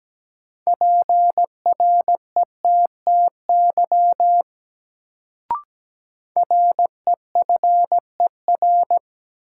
Morse code, heard as a keyed tone, about 17 wpm, 700 Hz. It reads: PRETTY REFER
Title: Sentences from Top 1000 Words - New Word Drill - 17wpm